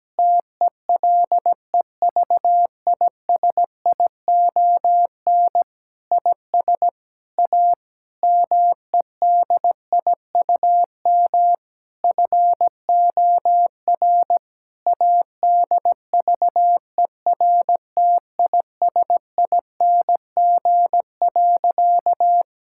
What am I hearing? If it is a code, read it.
Morse code, heard as a keyed tone, about 17 wpm, 700 Hz. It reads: TELEVISION IS A MEDIUM FOR ADVERTISING.
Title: Sentences from Top 1000 Words - Review Part 6 - 17wpm